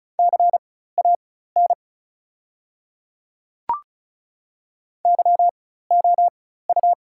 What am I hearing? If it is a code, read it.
Morse code, heard as a keyed tone, about 36 wpm, spaced as for 16 wpm, 690 Hz. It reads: CAN YOU
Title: 4x Character Spacing - Top 100 Words - 35wpm